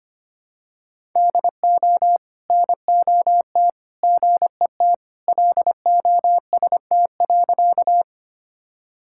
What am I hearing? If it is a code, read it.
Morse code, heard as a keyed tone, about 25 wpm, 700 Hz. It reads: DO NOT GET LOST.
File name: practice / 25wpm / Sentences with 3-Word Phrases - 25wpm